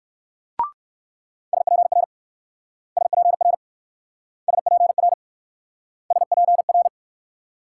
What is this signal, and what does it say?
Morse code, heard as a keyed tone, about 56 wpm, 700 Hz. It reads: SPR SPR SPR SPR
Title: Speed-Racing Multi-letter Phonemes - 45wpm